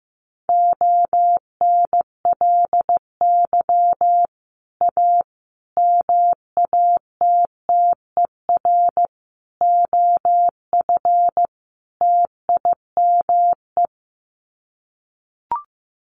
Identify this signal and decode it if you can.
Morse code, heard as a keyed tone, about 15 wpm, 700 Hz. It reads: ONLY A MATTER OF TIME
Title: Sets of 5 Words Encore 15wpm